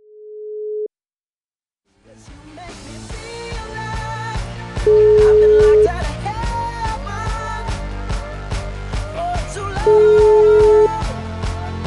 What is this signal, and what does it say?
Ringback tone (Europe, 425 Hz).